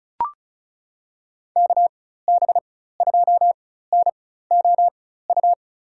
Morse code, heard as a keyed tone, about 36 wpm, spaced as for 16 wpm, 690 Hz. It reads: KB2NOU
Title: USA Call Signs - 6 Characters - 4x Character Spacing - 35wpm